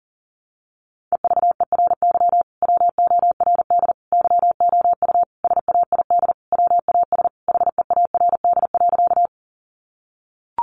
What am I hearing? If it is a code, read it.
Morse code, heard as a keyed tone, about 40 wpm, 700 Hz. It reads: EVERY WORD YOU SAID WAS HEARD.
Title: Sentences from Top 400 Words 40wpm